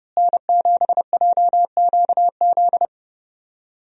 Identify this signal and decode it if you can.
Morse code, heard as a keyed tone, about 30 wpm, 700 Hz. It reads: N7JQZ
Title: USA Call Signs - 5 Characters - Repeated 3x - 30wpm